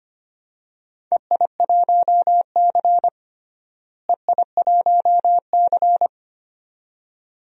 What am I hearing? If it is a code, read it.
Morse code, heard as a keyed tone, about 25 wpm, 700 Hz. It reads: EI1C EI1C
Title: DXCC-100 Call Signs - 4 Characters - Repeated 2x - 25wpm